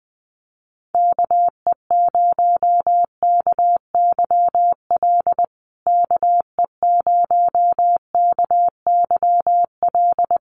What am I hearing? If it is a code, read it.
Morse code, heard as a keyed tone, about 20 wpm, 700 Hz. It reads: KE0KYL KE0KYL
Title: USA Call Signs Repeated 3x 20wpm